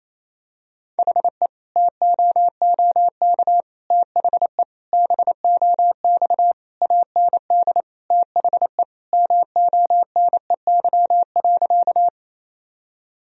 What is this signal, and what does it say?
Morse code, heard as a keyed tone, about 27 wpm, 700 Hz. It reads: HE TOOK THE BOX AND THE MONEY.
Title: Sentences from Top 1000 Words - Review Part 2 - 28wpm